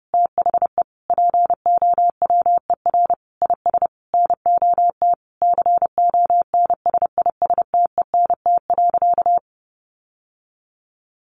Morse code, heard as a keyed tone, about 30 wpm, 700 Hz. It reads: THE POWER IS NOT CONSISTENT.
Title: Sentences from Top 1500 Words 30wpm